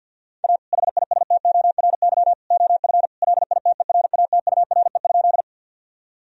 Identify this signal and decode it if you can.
Morse code, heard as a keyed tone, about 50 wpm, 700 Hz. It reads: A HISTORY OF LITERATURE?
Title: Sentences from Top 1000 Words - Review Part 3 - 50wpm